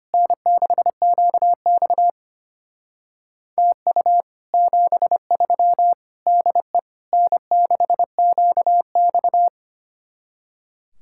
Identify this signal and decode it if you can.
Morse code, heard as a keyed tone, about 30 wpm, 700 Hz. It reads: N6QX TU 73 DE N6QX
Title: Speed-Racing Common QSO Phrases Encore - 25wpm